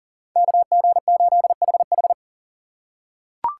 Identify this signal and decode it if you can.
Morse code, heard as a keyed tone, about 40 wpm, 700 Hz. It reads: KG8HH
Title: USA Call Signs - 2x Character Spacing - 40wpm